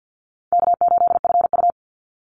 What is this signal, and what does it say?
Morse code, heard as a keyed tone, about 49 wpm, 700 Hz. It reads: K8FU
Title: Easy Call Signs Repeated 3x - 50wpm